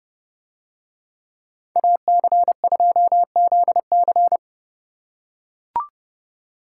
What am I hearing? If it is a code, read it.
Morse code, heard as a keyed tone, about 31 wpm, 700 Hz. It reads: AC2ZC